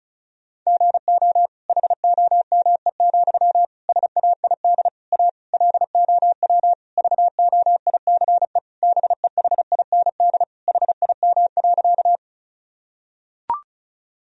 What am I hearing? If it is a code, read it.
Morse code, heard as a keyed tone, about 34 wpm, 700 Hz. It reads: GO HOME, SAID A LOW VOICE BEHIND HIM.
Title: Sentences from Top 1000 Words - Review Part 1 - 35wpm